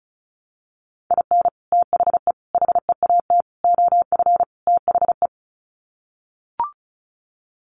Morse code, heard as a keyed tone, about 34 wpm, 700 Hz. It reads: IN THE HEAT OF THE
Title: Sets of 5 Words Encore2 35wpm